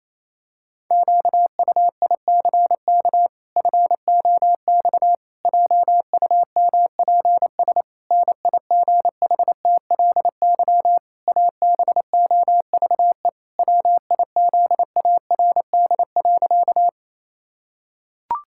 Morse code, heard as a keyed tone, about 28 wpm, 700 Hz. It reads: QUICK FOX JUMPS NIGHTLY ABOVE WIZARD.
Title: Pangram Challenge v2 - 28wpm